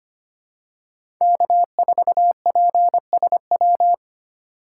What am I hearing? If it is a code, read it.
Morse code, heard as a keyed tone, about 25 wpm, 700 Hz. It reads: K4PSW
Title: USA Call Signs - 5 Characters - 4x Character Spacing - 25wpm